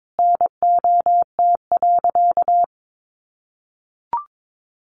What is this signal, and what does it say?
Morse code, heard as a keyed tone, about 22 wpm, 700 Hz. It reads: NOT.
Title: Sentences from Top 900 Words 22wpm